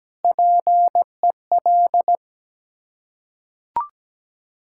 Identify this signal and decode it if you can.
Morse code, heard as a keyed tone, about 17 wpm, 700 Hz. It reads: PEL
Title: Ontario QSO Party Exchange 17wpm